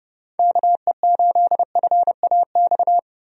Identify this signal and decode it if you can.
Morse code, heard as a keyed tone, about 30 wpm, 700 Hz. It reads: KE8FAX
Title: USA Call Signs - 6 Characters - Repeated 3x - 30wpm